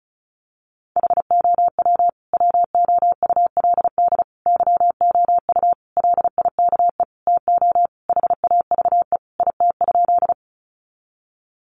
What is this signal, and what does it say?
Morse code, heard as a keyed tone, about 34 wpm, 700 Hz. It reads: HOW WOULD YOU LIKE TO HAVE IT?
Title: Sentences from Top 100 Words 35wpm